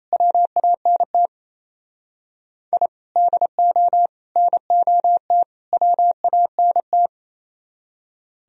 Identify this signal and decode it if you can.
Morse code, heard as a keyed tone, about 37 wpm, 700 Hz. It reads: WANT I DO NOT WANT